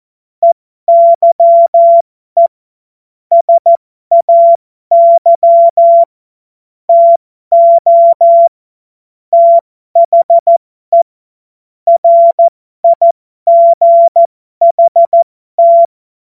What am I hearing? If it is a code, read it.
Morse code, heard as a keyed tone, about 14 wpm, spaced as for 12 wpm, 680 Hz. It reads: EYE SAY TO THE RIGHT
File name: Morsekurs / Witze